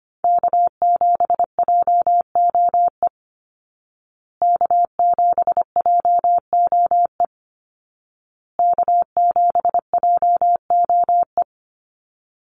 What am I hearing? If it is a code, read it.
Morse code, heard as a keyed tone, about 25 wpm, 700 Hz. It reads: K7JOE K7JOE K7JOE
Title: Call Signs Repeated 2x - 25wpm